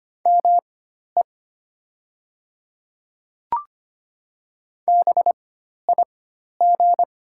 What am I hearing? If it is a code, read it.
Morse code, heard as a keyed tone, about 25 wpm, spaced as for 12 wpm, 700 Hz. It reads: ME BIG
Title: 4x Character Spacing - Top 300 Words - 25wpm